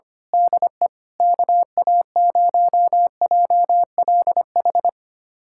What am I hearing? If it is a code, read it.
Morse code, heard as a keyed tone, about 25 wpm, 700 Hz. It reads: DE KA0JLH
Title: USA Callsign DE Callsign - 6 Characters - 25wpm